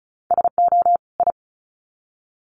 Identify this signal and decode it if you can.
Morse code, heard as a keyed tone, about 34 wpm, 700 Hz. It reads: SO I
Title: Sets of 2 Words 35wpm